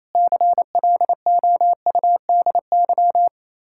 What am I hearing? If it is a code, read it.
Morse code, heard as a keyed tone, about 35 wpm, 700 Hz. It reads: CLOUDY
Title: Speed-Racing Common QSO Phrases - 28wpm